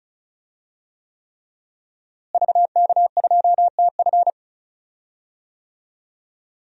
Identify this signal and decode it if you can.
Morse code, heard as a keyed tone, about 36 wpm, 700 Hz. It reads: VK2TF
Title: DXCC-100 Call Signs - 5 Characters - Repeated 3x - 35wpm